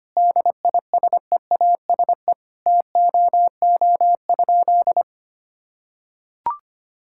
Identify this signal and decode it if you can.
Morse code, heard as a keyed tone, about 25 wpm, 700 Hz. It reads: DISEASE TOO?
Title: Sentences from Top 1000 Words - Review Part 6 - 25wpm